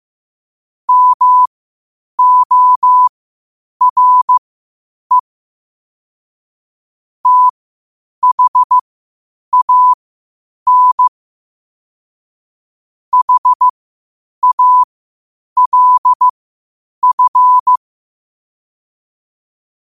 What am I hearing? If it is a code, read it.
Morse code, heard as a keyed tone, about 15 wpm, spaced as for 8 wpm, 1000 Hz. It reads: MORE THAN HALF
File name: qotdfiles / 8wpm